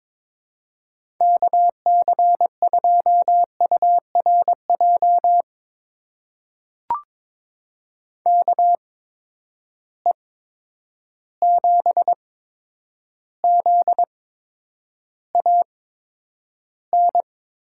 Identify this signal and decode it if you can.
Morse code, heard as a keyed tone, about 22 wpm, spaced as for 9 wpm, 700 Hz. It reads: KC2URJ KE7ZAN
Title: USA Call Signs - 6 Characters - 8x Character Spacing - 22wpm